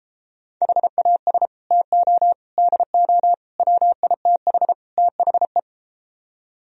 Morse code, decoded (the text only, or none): HAS TO DO WITH THE